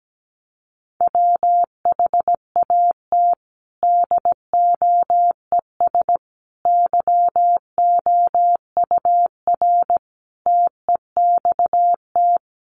Morse code, heard as a keyed tone, about 17 wpm, 700 Hz. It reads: WHAT DOES YOUR TEXT
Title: Sentences from Top 1200 Words 17wpm